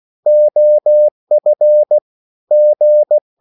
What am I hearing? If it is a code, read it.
Morse code, heard as a keyed tone, about 16 wpm, 600 Hz. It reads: OF G